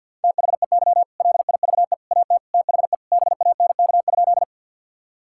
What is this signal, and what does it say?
Morse code, heard as a keyed tone, about 50 wpm, 700 Hz. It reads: THEY LIVE AT THE BANK?
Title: Sentences from Top 1000 Words - Review Part 6 - 50wpm